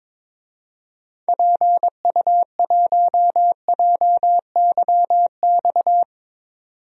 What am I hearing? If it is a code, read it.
Morse code, heard as a keyed tone, about 22 wpm, 700 Hz. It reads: PU1JYX
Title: DXCC-100 Call Signs - 6 Characters - 22wpm